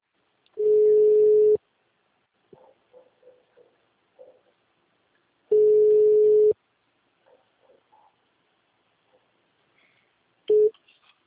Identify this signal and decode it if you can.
Ringback tone (Europe, 425 Hz).